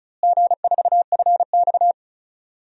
Morse code, heard as a keyed tone, about 36 wpm, 700 Hz. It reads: G4FX
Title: DXCC-100 Call Signs - 4 Characters - Repeated 3x - 35wpm